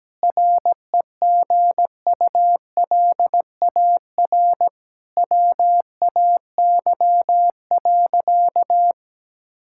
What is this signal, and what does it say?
Morse code, heard as a keyed tone, about 17 wpm, 700 Hz. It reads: REGULAR WAY.
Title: Sentences from Top 1000 Words - Review Part 6 - 17wpm